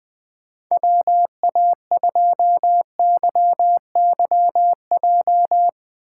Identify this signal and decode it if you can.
Morse code, heard as a keyed tone, about 20 wpm, 700 Hz. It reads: WA2YYJ
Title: USA Call Signs - 3x Character Spacing - 20wpm